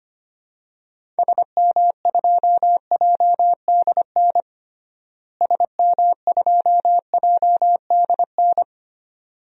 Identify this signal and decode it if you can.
Morse code, heard as a keyed tone, about 25 wpm, 700 Hz. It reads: SM2JDN SM2JDN
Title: DXCC-100 Call Signs - 6 Characters - Repeated 2x - 25wpm